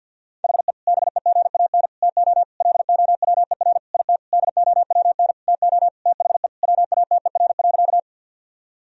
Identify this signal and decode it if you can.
Morse code, heard as a keyed tone, about 50 wpm, 700 Hz. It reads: HE BEGAN TO LOWER IT DOWN TO THE WATER.
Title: Sentences from Top 1000 Words - Review Part 4 - 50wpm